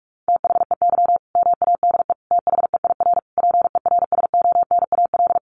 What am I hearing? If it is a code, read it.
Morse code, heard as a keyed tone, about 44 wpm, 700 Hz. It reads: THEY MADE THEIR PERSONAL
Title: Sentences from Top 1100 Words 45wpm